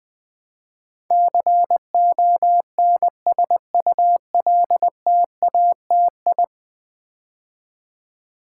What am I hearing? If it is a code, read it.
Morse code, heard as a keyed tone, about 20 wpm, 700 Hz. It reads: CONSULTATI
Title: Build A Word - 5 Letters At A Time - 12-Letter Words - 20wpm